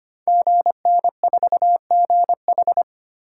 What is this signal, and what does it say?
Morse code, heard as a keyed tone, about 25 wpm, 700 Hz. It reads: GN4GH
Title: Code Groups - 5 Letter:Numbers - 25wpm